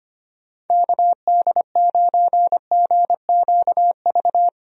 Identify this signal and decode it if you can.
Morse code, heard as a keyed tone, about 25 wpm, 700 Hz. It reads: KD9GQV
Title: USA Call Signs - 6 Characters - Repeated 3x - 25wpm